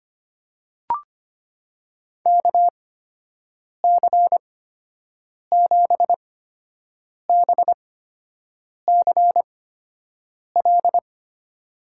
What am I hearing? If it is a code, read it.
Morse code, heard as a keyed tone, about 25 wpm, spaced as for 7 wpm, 700 Hz. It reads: KC7BCL
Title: USA Call Signs - 6 Characters - 8x Character Spacing - 25wpm